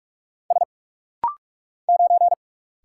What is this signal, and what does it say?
Morse code, heard as a keyed tone, about 46 wpm, 700 Hz. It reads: S 9
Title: Single Letter-Number - Rapid-Fire 45wpm